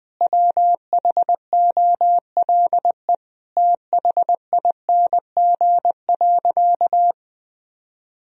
Morse code, heard as a keyed tone, about 20 wpm, 700 Hz. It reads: WHOLE THING.
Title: Sentences from Top 1000 Words - Review Part 1 - 20wpm